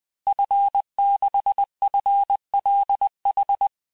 Morse code, heard as a keyed tone, about 20 wpm, 800 Hz. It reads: F6FLH
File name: cours-du-mercredi / audios / cours-870-FIN-de-QSO-vitesse-19-MPM